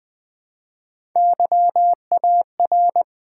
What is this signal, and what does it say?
Morse code, heard as a keyed tone, about 20 wpm, 700 Hz. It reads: YAR
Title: Build A Word - 1 Letter At A Time - 4-Letter Words - 20wpm